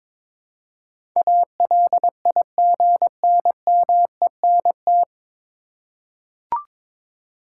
Morse code, heard as a keyed tone, about 22 wpm, 700 Hz. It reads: ALIGNMENT
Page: Morse Code Ninja practice 22wpm